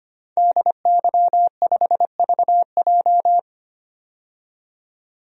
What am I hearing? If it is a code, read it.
Morse code, heard as a keyed tone, about 25 wpm, 700 Hz. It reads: DY5VJ
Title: DXCC-100 Call Signs - 5 Characters - 25wpm